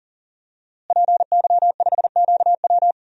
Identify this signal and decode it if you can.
Morse code, heard as a keyed tone, about 40 wpm, 700 Hz. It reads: PY5QW
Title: DXCC-100 Call Signs - 5 Characters - Repeated 3x - 40wpm